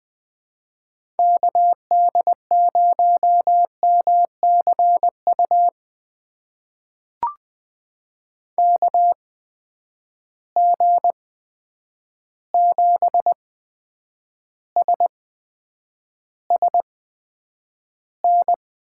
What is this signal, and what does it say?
Morse code, heard as a keyed tone, about 21 wpm, spaced as for 9 wpm, 700 Hz. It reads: KD0MCU KG7SSN
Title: USA Call Signs - 6 Characters - 8x Character Spacing - 20wpm